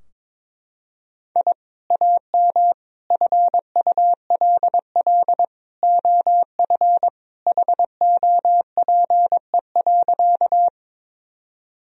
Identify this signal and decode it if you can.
Morse code, heard as a keyed tone, about 21 wpm, 700 Hz. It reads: I AM FULL OF HOPE.
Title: Sentences from Top 1000 Words - Review Part 3 - 22wpm